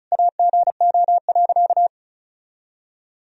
Morse code, heard as a keyed tone, about 36 wpm, 700 Hz. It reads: AGO.
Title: Sentences from Top 500 Words and Abbreviations 35wpm